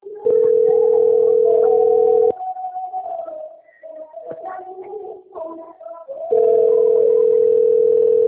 Ringback tone (North America), 440 + 480 Hz.